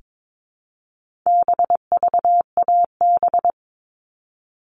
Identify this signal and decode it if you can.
Morse code, heard as a keyed tone, about 22 wpm, 700 Hz. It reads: BVAB
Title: All Cut Numbers - 4-Digit 22wpm